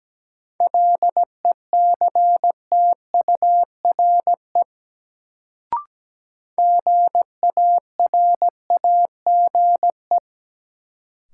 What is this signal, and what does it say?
Morse code, heard as a keyed tone, about 17 wpm, 700 Hz. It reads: LECTURE GARAGE
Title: Sentences from Top 1600 Words - New Word Drill Alt - 17wpm